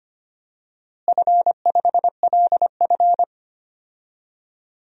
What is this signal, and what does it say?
Morse code, heard as a keyed tone, about 25 wpm, 700 Hz. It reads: F5LF
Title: DXCC-100 Call Signs - 4 Characters - Repeated 3x - 25wpm